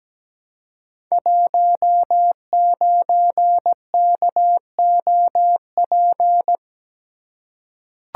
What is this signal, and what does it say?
Morse code, heard as a keyed tone, about 17 wpm, 700 Hz. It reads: 19KOP
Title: Code Groups - 5 Letter:Numbers - 17wpm